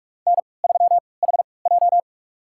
Morse code, heard as a keyed tone, about 48 wpm, spaced as for 26 wpm, 690 Hz. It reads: N3HJ
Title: USA Call Signs - 4 Characters - 3x Character Spacing - 45wpm